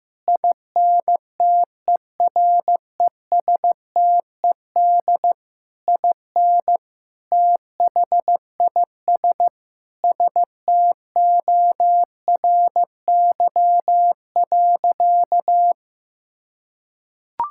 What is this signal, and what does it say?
Morse code, heard as a keyed tone, about 15 wpm, 700 Hz. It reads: INTERESTED IN THIS STORY.
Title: Sentences from Top 1600 Words 15wpm